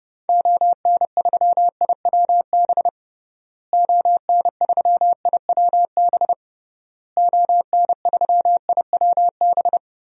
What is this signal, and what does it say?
Morse code, heard as a keyed tone, about 30 wpm, 700 Hz. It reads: ON3IWB ON3IWB ON3IWB
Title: DXCC-100 Call Signs - 6 Characters - Repeated 3x - 30wpm